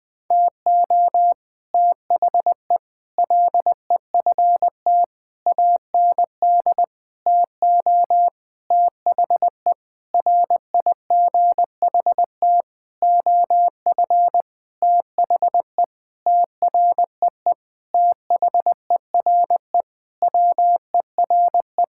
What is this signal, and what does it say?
Morse code, heard as a keyed tone, about 21 wpm, 700 Hz. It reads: TO THE LEFT AND TO THE RIGHT OF THE TREE THERE WERE
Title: Sentences with 9-Word Phrases - 20wpm